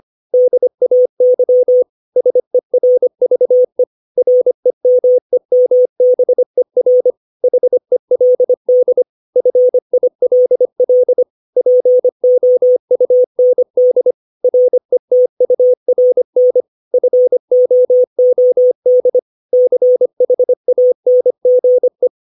Morse code, heard as a keyed tone, about 25 wpm, 500 Hz. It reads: DAY SERVE REMEMBER HELD FILL POUND RETURN FOOD CHANGE
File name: 632 words / 632 Words 25WPM